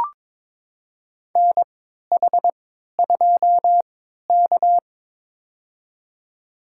Morse code, heard as a keyed tone, about 22 wpm, spaced as for 12 wpm, 700 Hz. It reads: NH2K
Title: USA Call Signs - 4 Characters - 3x Character Spacing - 22wpm